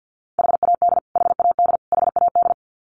Morse code, heard as a keyed tone, about 49 wpm, 700 Hz. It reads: HAD HAD HAD
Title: Top 500 Words Repeated 50wpm